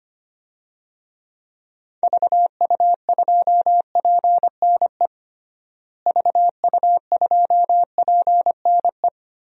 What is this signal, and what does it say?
Morse code, heard as a keyed tone, about 25 wpm, 700 Hz. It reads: VU2PNE VU2PNE
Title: DXCC-100 Call Signs - 6 Characters - Repeated 2x - 25wpm